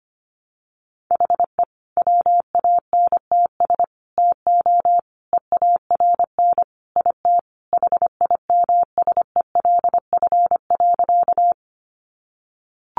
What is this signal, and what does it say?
Morse code, heard as a keyed tone, about 25 wpm, 700 Hz. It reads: HE WANTS TO EARN IT HIMSELF.
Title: Additional Easy Phrases 25wpm